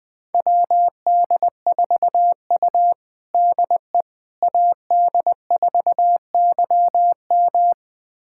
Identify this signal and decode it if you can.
Morse code, heard as a keyed tone, about 20 wpm, 700 Hz. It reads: WD4U DE AD4YM
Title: USA Callsign DE Callsign 20wpm